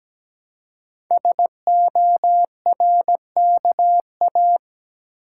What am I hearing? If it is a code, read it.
Morse code, heard as a keyed tone, about 17 wpm, 700 Hz. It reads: SORKA
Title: Code Groups - 5 Letters - 17wpm